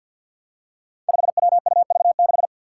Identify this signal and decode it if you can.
Morse code, heard as a keyed tone, about 50 wpm, 700 Hz. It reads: 5WUV6
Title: Five Challenging Letter-Number Combinations 50wpm